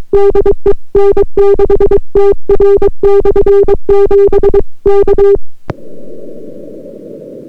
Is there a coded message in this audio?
Yes: Morse code from 0.1 to 5.4 s, heard as a keyed tone, about 22 wpm, 400 Hz. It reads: DE N6TR/7 K